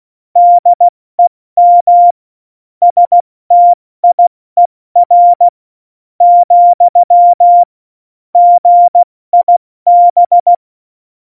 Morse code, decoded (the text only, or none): DEM STIER , GIB